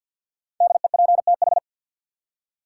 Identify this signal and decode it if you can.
Morse code, heard as a keyed tone, about 52 wpm, 700 Hz. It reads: DEPTH
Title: Five Letter Words - 50wpm